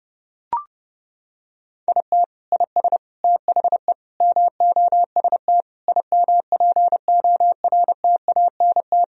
Morse code, heard as a keyed tone, about 32 wpm, 700 Hz. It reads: IT IS THE MOST IMPORTANT
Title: Sentences from Top 1000 Words - Review Part 1 - 30wpm